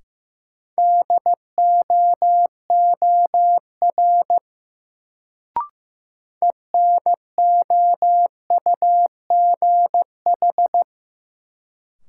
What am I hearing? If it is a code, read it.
Morse code, heard as a keyed tone, about 15 wpm, 700 Hz. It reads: DOOR ENOUGH
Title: Sets of 2 Words - New Word Drill - 15wpm